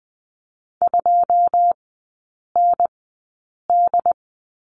Morse code, heard as a keyed tone, about 20 wpm, spaced as for 8 wpm, 700 Hz. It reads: 2ND
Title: 087 Lesson 15e - Number 5 - Three Characters 20wpm